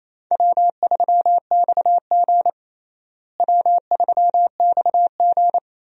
Morse code, heard as a keyed tone, about 28 wpm, 700 Hz. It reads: W3XG W3XG
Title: USA Call Signs - 4 Characters - Repeated 3x - 28wpm